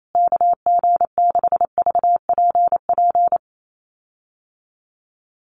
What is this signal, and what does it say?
Morse code, heard as a keyed tone, about 28 wpm, 700 Hz. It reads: KG6VPP
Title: USA Call Signs Repeated 3x 28wpm